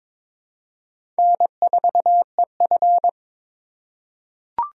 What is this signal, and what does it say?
Morse code, heard as a keyed tone, about 22 wpm, 700 Hz. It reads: N4EF
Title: USA Call Signs - 4 Characters - 3x Character Spacing - 22wpm